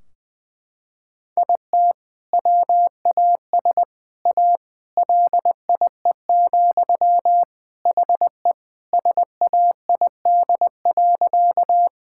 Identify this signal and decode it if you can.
Morse code, heard as a keyed tone, about 20 wpm, 700 Hz. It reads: IT WAS A LIE, HE SAID.
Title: Sentences from Top 1600 Words 20wpm